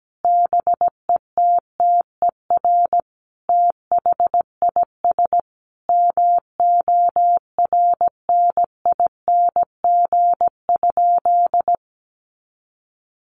Morse code, heard as a keyed tone, about 17 wpm, 700 Hz. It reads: BETTER THIS MORNING?
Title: Sentences from Top 1200 Words 17wpm